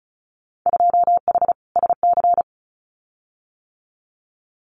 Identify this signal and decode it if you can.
Morse code, heard as a keyed tone, about 36 wpm, 690 Hz. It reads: 2H SC
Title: Winter Field Day Exchanges 35wpm